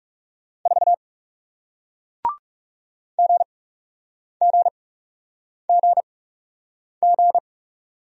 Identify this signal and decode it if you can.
Morse code, heard as a keyed tone, about 40 wpm, spaced as for 8 wpm, 700 Hz. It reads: 4 GGGG